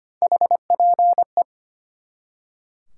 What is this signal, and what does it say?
Morse code, heard as a keyed tone, about 32 wpm, 700 Hz. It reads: HPE